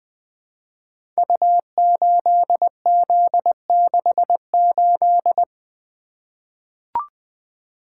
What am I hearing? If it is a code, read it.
Morse code, heard as a keyed tone, about 20 wpm, 700 Hz. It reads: U8Z68
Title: Code Groups - 5 Letter:Numbers - 20wpm